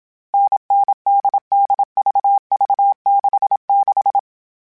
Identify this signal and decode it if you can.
Morse code, heard as a keyed tone, about 27 wpm, 800 Hz. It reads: NNDDVV66
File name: Variable Speed / Variable Speed-mp3s / GZ GZ7 sp @25